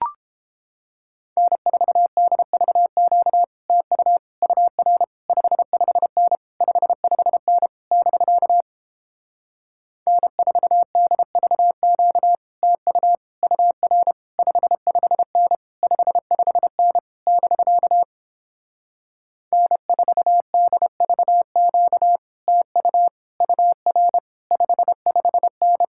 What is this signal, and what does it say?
Morse code, heard as a keyed tone, about 31 wpm, 700 Hz. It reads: N4DVQ TU UR 55N 55N <BK> N4DVQ TU UR 55N 55N <BK> N4DVQ TU UR 55N